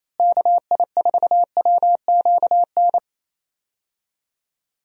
Morse code, heard as a keyed tone, about 28 wpm, 700 Hz. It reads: KI4WQN
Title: USA Call Signs - 6 Characters - Repeated 2x - 28wpm